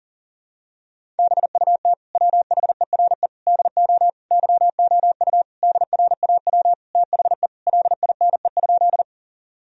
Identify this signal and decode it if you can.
Morse code, heard as a keyed tone, about 40 wpm, 700 Hz. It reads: BUT WHERE DO YOU DRAW THE LINE?
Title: Sentences from Top 1000 Words - Review Part 1 - 40wpm